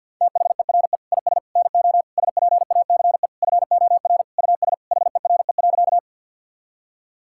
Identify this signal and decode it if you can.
Morse code, heard as a keyed tone, about 50 wpm, 700 Hz. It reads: THERE IS NO SPACE FOR US HERE.
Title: Sentences from Top 1000 Words - Review Part 5 - 50wpm